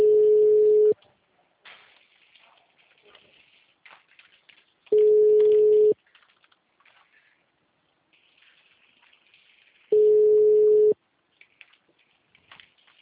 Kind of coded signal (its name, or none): telephone call progress tone